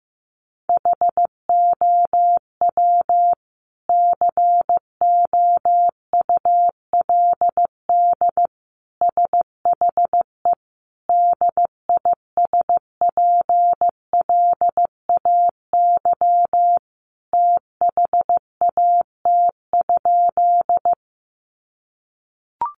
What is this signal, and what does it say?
Morse code, heard as a keyed tone, about 15 wpm, 700 Hz. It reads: HOW COULD SHE DISPLAY THAT?
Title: Sentences from Top 1200 Words 15wpm